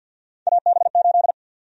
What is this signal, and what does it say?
Morse code, heard as a keyed tone, about 53 wpm, 700 Hz. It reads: AB8
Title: USA Call Sign Prefixes - 50wpm